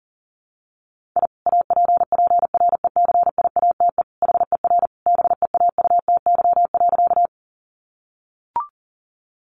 Morse code, heard as a keyed tone, about 40 wpm, 700 Hz. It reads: I APPRECIATE HER BEAUTY.
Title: Sentences from Top 1300 Words 40wpm